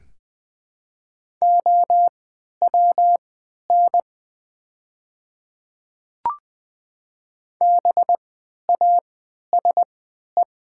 Morse code, heard as a keyed tone, about 20 wpm, spaced as for 11 wpm, 700 Hz. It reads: OWN BASE
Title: 3x Character Spacing - Top 300 Words - 20wpm